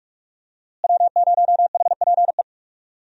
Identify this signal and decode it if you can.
Morse code, heard as a keyed tone, about 46 wpm, 700 Hz. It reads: W0HPE
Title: USA Call Signs - 5 Characters - Repeated 3x - 45wpm